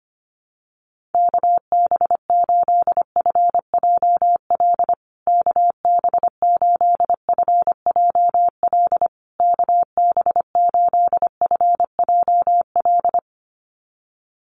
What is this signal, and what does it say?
Morse code, heard as a keyed tone, about 25 wpm, 700 Hz. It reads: KB8FJL KB8FJL KB8FJL
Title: USA Call Signs Repeated 3x 25wpm